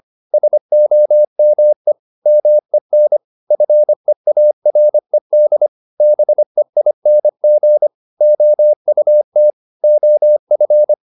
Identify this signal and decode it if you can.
Morse code, heard as a keyed tone, about 25 wpm, 600 Hz. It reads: SOME MEN FEARED BEING OUT OF
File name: the old man and the sea part 1 25wpm